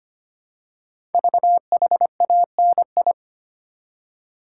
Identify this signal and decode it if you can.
Morse code, heard as a keyed tone, about 25 wpm, 700 Hz. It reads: VHANI
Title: Code Groups - 5 Letters - 25wpm